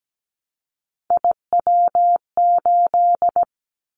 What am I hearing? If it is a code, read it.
Morse code, heard as a keyed tone, about 17 wpm, 700 Hz. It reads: IW8
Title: Callsign Prefixes for Top 50 DXCC - 17wpm